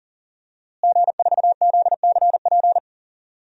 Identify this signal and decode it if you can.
Morse code, heard as a keyed tone, about 37 wpm, 700 Hz. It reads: G4ZCP
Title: DXCC-100 Call Signs - 5 Characters - Repeated 2x - 40wpm